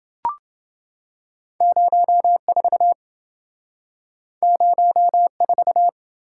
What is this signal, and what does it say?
Morse code, heard as a keyed tone, about 28 wpm, 700 Hz. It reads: 04 04